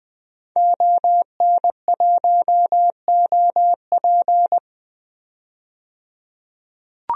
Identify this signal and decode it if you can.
Morse code, heard as a keyed tone, about 20 wpm, 700 Hz. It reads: ON1OP
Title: DXCC-100 Call Signs - 5 Characters - 20wpm